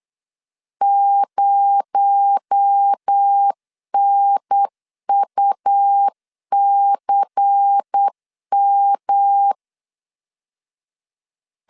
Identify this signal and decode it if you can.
Morse code, heard as a keyed tone, about 8 wpm, 800 Hz. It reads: 0NUCM
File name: CW / DET / 1 6 1500C S B 8wpm 5142